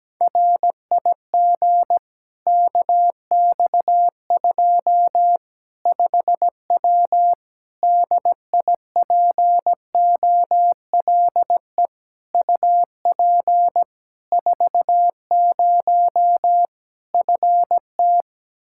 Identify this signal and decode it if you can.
Morse code, heard as a keyed tone, about 17 wpm, 700 Hz. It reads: RIG KX2 5W DIPOLE UP 40 FT
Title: Practice QSO - Exchange 2 - Info - 17wpm